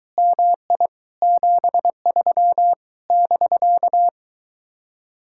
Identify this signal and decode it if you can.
Morse code, heard as a keyed tone, about 22 wpm, 700 Hz. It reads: MI 73 <BK>